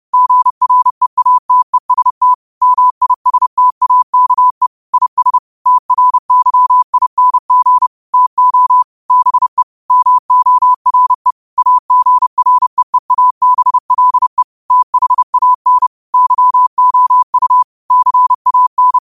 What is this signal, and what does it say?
Morse code, heard as a keyed tone, about 30 wpm, 1000 Hz. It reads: GREATEST MISTAKE IS TRYING TO BE MORE AGREEABLE THAN YOU CAN
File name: qotdfiles / 30wpm